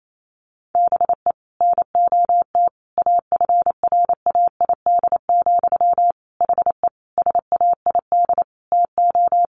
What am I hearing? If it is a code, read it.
Morse code, heard as a keyed tone, about 28 wpm, 700 Hz. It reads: BE NOT AFRAID, HE SAID TO
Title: Sentences from Top 1500 Words 28wpm